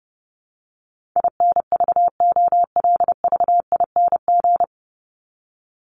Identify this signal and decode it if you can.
Morse code, heard as a keyed tone, about 30 wpm, 700 Hz. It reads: INVOLVING
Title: ING Suffix - Top 100 Words with Base Words 30wpm